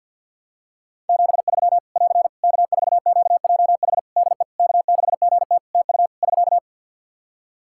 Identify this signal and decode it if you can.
Morse code, heard as a keyed tone, about 48 wpm, 700 Hz. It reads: 73 <AR> K4QJH DE K6GT TU <SK>
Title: Practice QSO - Exchange 4 - Ending - 50wpm